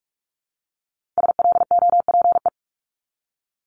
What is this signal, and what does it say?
Morse code, heard as a keyed tone, about 46 wpm, 700 Hz. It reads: SLOPE
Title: Multi-letter Phonemes Repeated 3x 45wpm